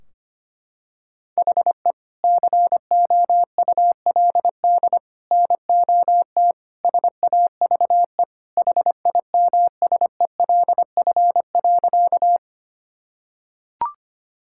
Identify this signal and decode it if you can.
Morse code, heard as a keyed tone, about 25 wpm, 700 Hz. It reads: HE COULD NOT SAVE HIMSELF.
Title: Sentences from Top 1600 Words 25wpm